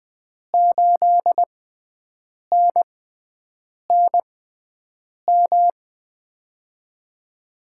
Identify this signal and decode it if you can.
Morse code, heard as a keyed tone, about 20 wpm, spaced as for 7 wpm, 700 Hz. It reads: 8NNM
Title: USA Call Signs - 6x Character Spacing - 20wpm